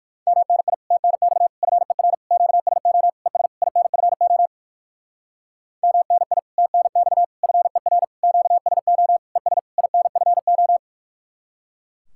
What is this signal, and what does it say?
Morse code, heard as a keyed tone, about 58 wpm, 700 Hz. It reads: MNI TNX FER QSO ES INFO MNI TNX FER QSO ES INFO